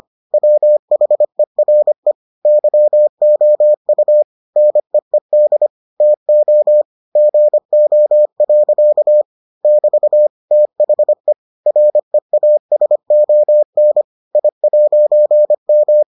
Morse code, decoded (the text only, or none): WHERE YOU NEED TO GO. <BT> THE REASON I'M